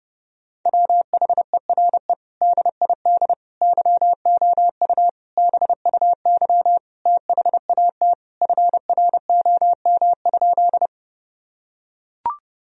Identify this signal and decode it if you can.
Morse code, heard as a keyed tone, about 28 wpm, 700 Hz. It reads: WHERE DID YOU BUY THAT FROM?
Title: Sentences from Top 1000 Words - Review Part 4 - 30wpm